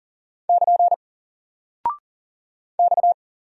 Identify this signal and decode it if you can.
Morse code, heard as a keyed tone, about 40 wpm, 700 Hz. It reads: <KN> X
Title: Single Character (Letter:Number:Punctuation:Prosign) Alt 40wpm